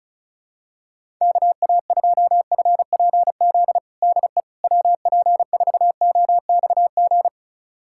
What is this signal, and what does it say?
Morse code, heard as a keyed tone, about 34 wpm, 700 Hz. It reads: KA2FPZ DE WP4OXG
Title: USA Callsign DE Callsign - 6 Characters - 35wpm